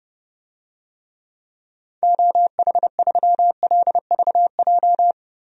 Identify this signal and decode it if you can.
Morse code, heard as a keyed tone, about 30 wpm, 700 Hz. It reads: OH3LVJ
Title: DXCC-100 Call Signs - 6 Characters - 30wpm